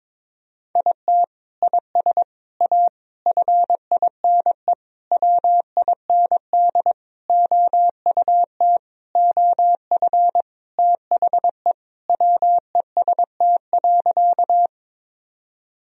Morse code, heard as a keyed tone, about 22 wpm, 700 Hz. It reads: IT IS A FINE WIND OUT OF THE WEST.
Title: Sentences from Top 1000 Words - Review Part 2 - 22wpm